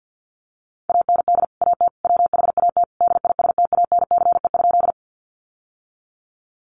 Morse code, heard as a keyed tone, about 50 wpm, 700 Hz. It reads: AND AT WHAT DISTANCE?
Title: Sentences from Top 1200 Words 50wpm